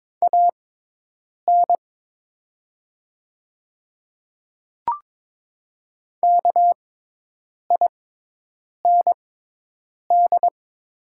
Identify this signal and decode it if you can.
Morse code, heard as a keyed tone, about 22 wpm, spaced as for 8 wpm, 700 Hz. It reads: AN KIND
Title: 6x Character Spacing - Top 200 Words - 22wpm